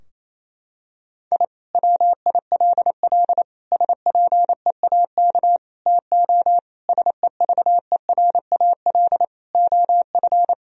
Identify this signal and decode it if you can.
Morse code, heard as a keyed tone, about 28 wpm, 700 Hz. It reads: I WILL SPEAK TO SEVERAL OF
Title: Sentences from Top 1000 Words - Review Part 2 - 28wpm